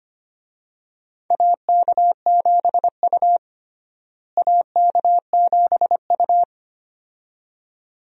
Morse code, heard as a keyed tone, about 25 wpm, 700 Hz. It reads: AK7U AK7U
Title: USA Call Signs - 4 Characters - Repeated 2x - 25wpm